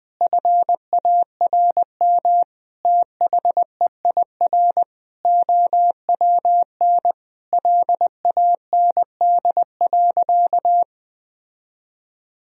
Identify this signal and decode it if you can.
Morse code, heard as a keyed tone, about 21 wpm, 700 Hz. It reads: FARM THEIR OWN LAND.
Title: Sentences from Top 1000 Words - Review Part 2 - 20wpm